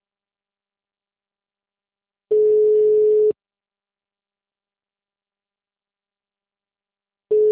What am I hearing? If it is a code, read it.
Ringback tone (Europe, 425 Hz).